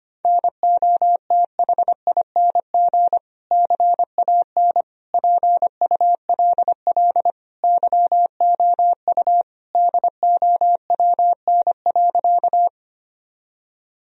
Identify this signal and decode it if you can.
Morse code, heard as a keyed tone, about 25 wpm, 700 Hz. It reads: NOTHING CAN PULL YOU DOWN.
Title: Sentences from Top 1000 Words - Review Part 2 - 25wpm